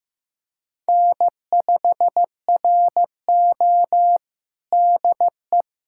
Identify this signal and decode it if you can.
Morse code, heard as a keyed tone, about 15 wpm, 700 Hz. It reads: N5RO DE
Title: USA Callsign DE Callsign - 4 Characters - 15wpm